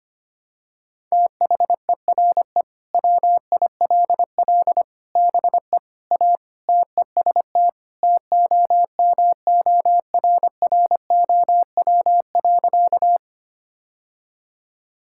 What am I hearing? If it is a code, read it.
Morse code, heard as a keyed tone, about 25 wpm, 700 Hz. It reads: THERE WILL BE A TEST TOMORROW.
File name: practice / 25wpm / Sentences with 3-Word Phrases - 25wpm